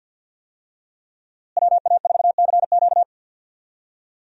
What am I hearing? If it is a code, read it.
Morse code, heard as a keyed tone, about 51 wpm, 700 Hz. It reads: WA4CQ
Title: USA Call Signs - 5 Characters - Repeated 2x - 50wpm